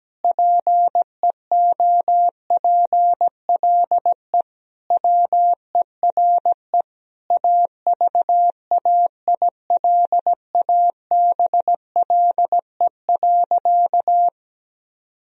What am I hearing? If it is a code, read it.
Morse code, heard as a keyed tone, about 17 wpm, 700 Hz. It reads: PEOPLE WERE AVAILABLE.
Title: Sentences from Top 1000 Words - Review Part 3 - 17wpm